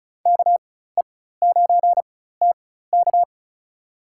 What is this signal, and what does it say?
Morse code, heard as a keyed tone, about 36 wpm, spaced as for 16 wpm, 700 Hz. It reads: KE9TK
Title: USA Call Signs - 4x Character Spacing - 35wpm